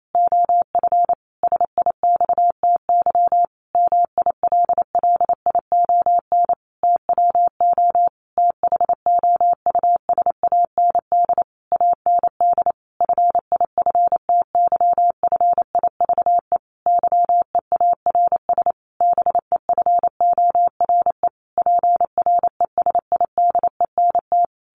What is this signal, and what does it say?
Morse code, heard as a keyed tone, about 28 wpm, 700 Hz. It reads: OF SIXTY MILLION TWO THOUSAND AND FIFTYFIVE YEARS BEFORE PRESIDENT
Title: A Sound Of Thunder 28wpm